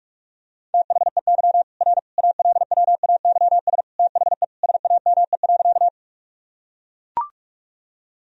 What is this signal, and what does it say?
Morse code, heard as a keyed tone, about 41 wpm, 700 Hz. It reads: THEY R ALWAYS THE SAME.
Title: Sentences from Top 300 Words and Abbreviations 45wpm